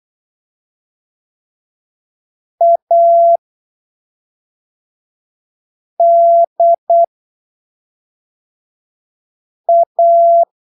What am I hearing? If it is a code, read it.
Morse code, heard as a keyed tone, about 8 wpm, spaced as for 3 wpm, 680 Hz. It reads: ADA
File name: arquivos / texto / texto 10 08ppm